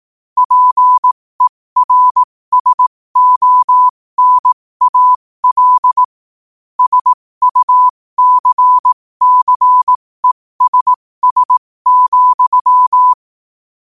Morse code, heard as a keyed tone, about 18 wpm, spaced as for 15 wpm, 1000 Hz. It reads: PERSONAL SUCCESS,
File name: qotdfiles / 15wpm